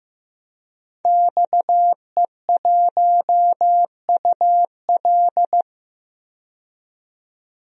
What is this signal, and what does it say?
Morse code, heard as a keyed tone, about 16 wpm, 700 Hz. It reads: XE1UL
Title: DXCC-100 Call Signs - 5 Characters - 15wpm